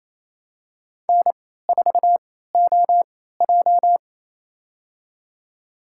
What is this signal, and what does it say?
Morse code, heard as a keyed tone, about 27 wpm, spaced as for 16 wpm, 700 Hz. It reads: N4OJ
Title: USA Call Signs - 3x Character Spacing - 28wpm